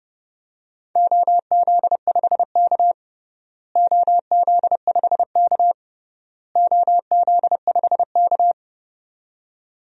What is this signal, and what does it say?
Morse code, heard as a keyed tone, about 30 wpm, 700 Hz. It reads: OZ5K OZ5K OZ5K
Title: DXCC-100 Call Signs - 4 Characters - Repeated 3x - 30wpm